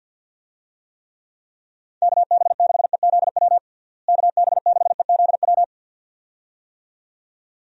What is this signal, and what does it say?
Morse code, heard as a keyed tone, about 51 wpm, 700 Hz. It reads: KB6EZW KB6EZW
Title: USA Call Signs - 6 Characters - Repeated 2x - 50wpm